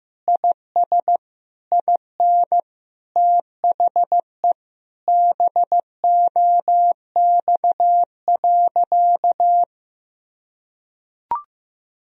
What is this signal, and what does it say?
Morse code, heard as a keyed tone, about 16 wpm, 700 Hz. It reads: IS IN THE BOX.
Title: Sentences from Top 1000 Words - Review Part 2 - 15wpm